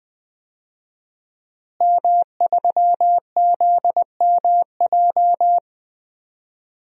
Morse code, heard as a keyed tone, about 21 wpm, 700 Hz. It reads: M3ZMJ
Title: DXCC-100 Call Signs - 5 Characters - Repeated 3x - 20wpm